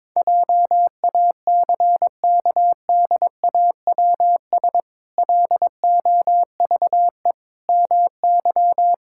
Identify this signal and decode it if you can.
Morse code, heard as a keyed tone, about 21 wpm, 700 Hz. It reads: JACKDAWS LOVE MY
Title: Pangram Challenge v2 - 22wpm